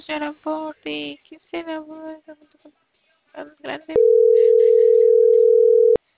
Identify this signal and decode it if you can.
Ringback tone (North America), 440 + 480 Hz, with speech.
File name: CENTRAL CAEA / out agcaea01 - .1981